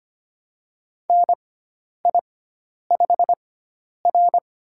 Morse code, heard as a keyed tone, about 25 wpm, spaced as for 10 wpm, 700 Hz. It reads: NI5R
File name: Lesson 16g - Letter c - Call Signs 25wpm